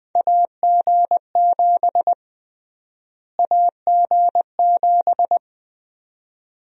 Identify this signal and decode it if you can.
Morse code, heard as a keyed tone, about 20 wpm, 700 Hz. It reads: AG7 AG7
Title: USA Call Sign Prefixes - Repeated 2x - 20wpm